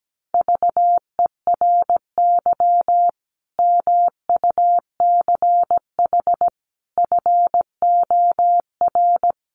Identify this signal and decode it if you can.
Morse code, heard as a keyed tone, about 17 wpm, 700 Hz. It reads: VERY MUCH FOR
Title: Sets of 5 Words 17wpm